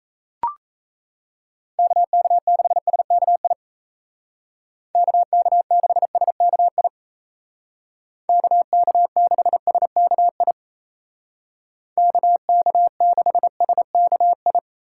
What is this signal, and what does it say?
Morse code, heard as a keyed tone, about 34 wpm, 700 Hz. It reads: KK6SKI KK6SKI KK6SKI KK6SKI
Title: Speed Racing - USA Call Signs 28wpm